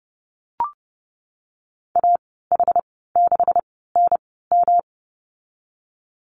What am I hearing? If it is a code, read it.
Morse code, heard as a keyed tone, about 32 wpm, spaced as for 17 wpm, 700 Hz. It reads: AH6NM
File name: Lesson 37g - Letter x - Call Signs 30wpm